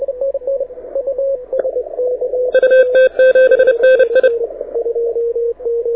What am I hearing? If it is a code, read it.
Morse code, heard as a keyed tone, about 30 wpm, 1550 Hz, with radio noise. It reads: UT7NI
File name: audio zawody / IARU 2015 CW SN7Q